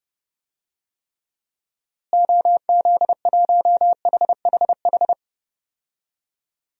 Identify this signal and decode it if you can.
Morse code, heard as a keyed tone, about 30 wpm, 700 Hz. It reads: OZ1HHH
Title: DXCC-100 Call Signs - 6 Characters - Repeated 2x - 30wpm